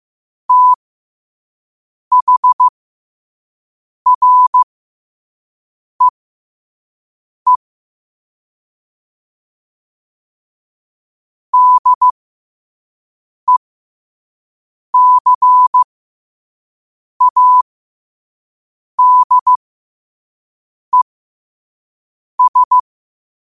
Morse code, heard as a keyed tone, about 15 wpm, spaced as for 5 wpm, 1000 Hz. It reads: THREE DECADES
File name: qotdfiles / 5wpm